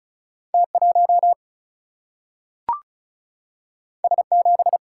Morse code, heard as a keyed tone, about 36 wpm, 700 Hz. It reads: T 1 S 7
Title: Single-Letter Single-Number 35wpm